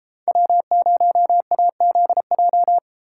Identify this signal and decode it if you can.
Morse code, heard as a keyed tone, about 32 wpm, 700 Hz. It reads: W0AZJ